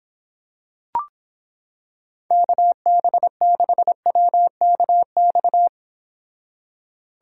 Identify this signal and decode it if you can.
Morse code, heard as a keyed tone, about 26 wpm, 700 Hz. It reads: KB6WKX